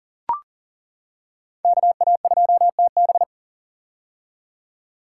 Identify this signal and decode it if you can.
Morse code, heard as a keyed tone, about 40 wpm, 700 Hz. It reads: KA2TB
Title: Code Groups - 5 Letter:Numbers - 40wpm